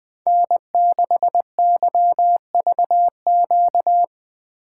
Morse code, heard as a keyed tone, about 20 wpm, 700 Hz. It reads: N6YVQ
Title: USA Call Signs - 5 Characters - 4x Character Spacing - 20wpm